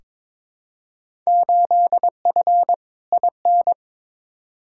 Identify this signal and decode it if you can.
Morse code, heard as a keyed tone, about 22 wpm, 700 Hz. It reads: 8F IN
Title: Field Day Exchanges Encore - 22wpm